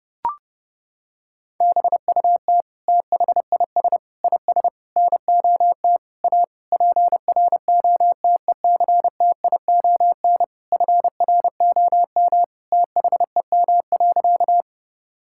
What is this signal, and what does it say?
Morse code, heard as a keyed tone, about 30 wpm, 700 Hz. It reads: BUT THIS IS NOT A PROTECTION FROM THEM.
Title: Sentences from Top 1600 Words 30wpm